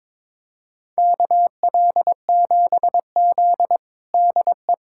Morse code, heard as a keyed tone, about 22 wpm, 700 Hz. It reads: KL7Z DE
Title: USA Callsign DE Callsign - 4 Characters - 22wpm